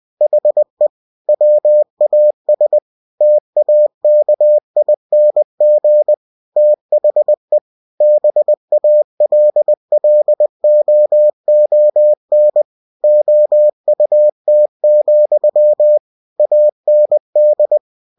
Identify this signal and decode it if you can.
Morse code, heard as a keyed tone, about 20 wpm, 600 Hz. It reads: HE WAS TAKING THE BALLOON OUT, AND
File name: cw-books / winnie-the-pooh / winnie-20wpm-page-0058